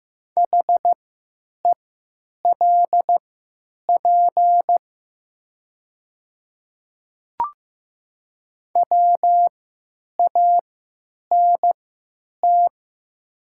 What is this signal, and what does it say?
Morse code, heard as a keyed tone, about 15 wpm, spaced as for 9 wpm, 700 Hz. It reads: HELP WANT
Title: 3x Character Spacing - Top 300 Words - 15wpm